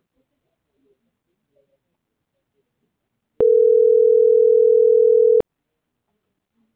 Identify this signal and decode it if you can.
Ringback tone (North America), 440 + 480 Hz.
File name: CENTRAL TERRESTRE / out karis - .77673